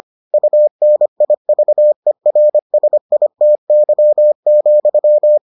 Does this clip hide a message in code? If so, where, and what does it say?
Yes: Morse code from 0.3 to 5.4 s, heard as a keyed tone, about 25 wpm, 600 Hz. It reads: UNIVERSITY,